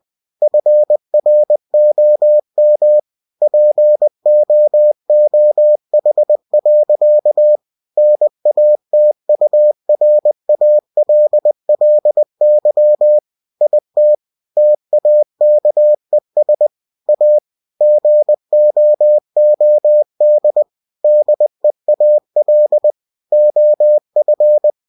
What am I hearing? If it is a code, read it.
Morse code, heard as a keyed tone, about 20 wpm, 600 Hz. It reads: FROM POOH. NATURALLY IT TAKES A GOOD DEAL OF